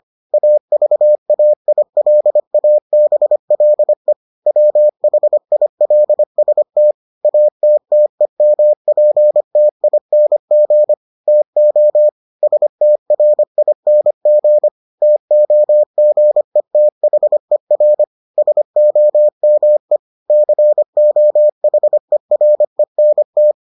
Morse code, heard as a keyed tone, about 25 wpm, 600 Hz. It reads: AVAILABLE WHILST ATTEMPTING TO STRING TOGETHER SOME COHERENT